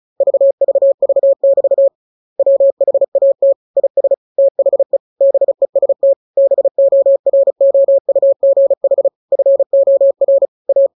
Morse code, heard as a keyed tone, about 34 wpm, 550 Hz. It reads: VVV<BT> WHAT IS THE BEST BOROUGH FOR A
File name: jokes / joke 15 35 0000